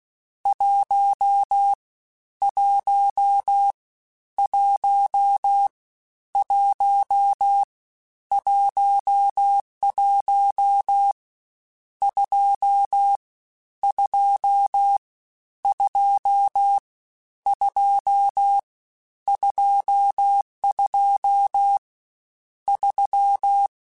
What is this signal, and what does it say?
Morse code, heard as a keyed tone, about 16 wpm, 800 Hz. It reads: 1 1 1 1 11 2 2 2 2 22 3
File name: Misc / Misc-mp3s / Numb @15